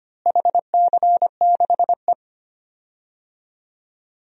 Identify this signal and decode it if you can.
Morse code, heard as a keyed tone, about 25 wpm, 700 Hz. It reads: HC6E
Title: DXCC-100 Call Signs - 4 Characters - Repeated 3x - 25wpm